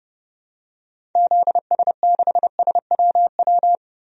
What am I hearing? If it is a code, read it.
Morse code, heard as a keyed tone, about 30 wpm, 700 Hz. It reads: ZS6SWW
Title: DXCC-100 Call Signs - 6 Characters - 30wpm